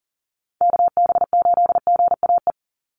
Morse code, heard as a keyed tone, about 41 wpm, 700 Hz. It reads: KB8GAE
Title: Call Signs - 40wpm